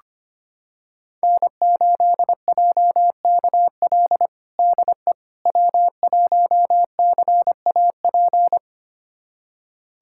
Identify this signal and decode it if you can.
Morse code, heard as a keyed tone, about 25 wpm, 700 Hz. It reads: N8JKL DE W1CAP
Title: USA Callsign DE Callsign - 5 Characters - 25wpm